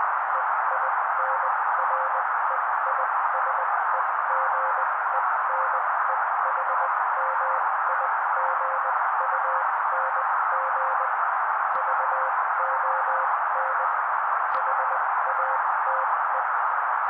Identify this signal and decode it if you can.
Morse code, heard as a keyed tone, about 20 wpm, spaced as for 16 wpm, 530 Hz. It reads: EINREISEGENEHMIGUNG VON HATE
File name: daily / 16wpm noise